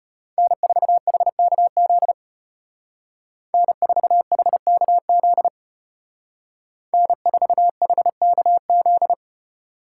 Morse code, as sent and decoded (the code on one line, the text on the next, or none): -. ....- .... -.- --.. / -. ....- .... -.- --.. / -. ....- .... -.- --..
N4HKZ N4HKZ N4HKZ